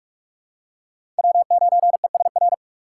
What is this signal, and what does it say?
Morse code, heard as a keyed tone, about 43 wpm, 700 Hz. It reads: W9ESR
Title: USA Call Signs - 6x Character Spacing - 45wpm